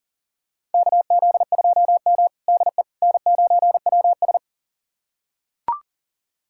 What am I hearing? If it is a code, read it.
Morse code, heard as a keyed tone, about 37 wpm, 700 Hz. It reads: KZ2M DE N9WS
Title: USA Callsign DE Callsign - 4 Characters - 40wpm